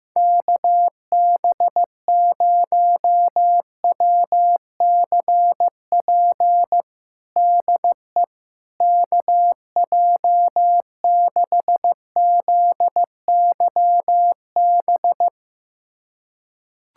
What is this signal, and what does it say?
Morse code, heard as a keyed tone, about 15 wpm, 700 Hz. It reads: KB0WCP DE KJ6ZYB
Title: USA Callsign DE Callsign - 6 Characters - 15wpm